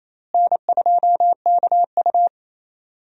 Morse code, heard as a keyed tone, about 28 wpm, 700 Hz. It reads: N2KU
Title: USA Call Signs - 4 Characters - Repeated 2x - 28wpm